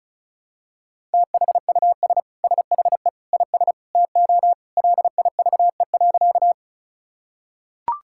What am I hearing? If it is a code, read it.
Morse code, heard as a keyed tone, about 36 wpm, 700 Hz. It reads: THUS SHE IS TO LIVE.
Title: Sentences from Top 1000 Words - Review Part 5 - 35wpm